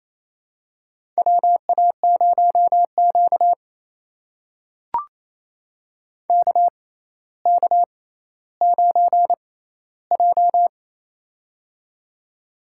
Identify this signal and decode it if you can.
Morse code, heard as a keyed tone, about 28 wpm, spaced as for 14 wpm, 700 Hz. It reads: WA0Q KK9J
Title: USA Call Signs - 4 Characters - 6x Character Spacing - 28wpm